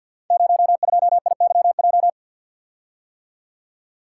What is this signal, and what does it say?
Morse code, heard as a keyed tone, about 52 wpm, 700 Hz. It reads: 02IYJ
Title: Code Groups - 5 Letter:Numbers - 50wpm